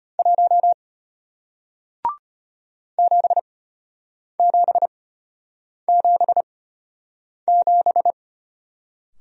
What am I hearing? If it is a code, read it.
Morse code, heard as a keyed tone, about 33 wpm, spaced as for 8 wpm, 700 Hz. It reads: 1 7777